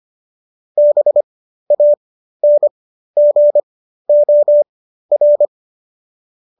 Morse code, heard as a keyed tone, about 25 wpm, spaced as for 13 wpm, 590 Hz. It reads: BANGOR